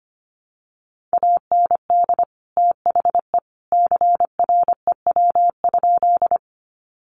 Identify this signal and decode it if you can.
Morse code, heard as a keyed tone, about 25 wpm, 700 Hz. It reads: AND THE CREW?
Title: Sentences from Top 1500 Words 25wpm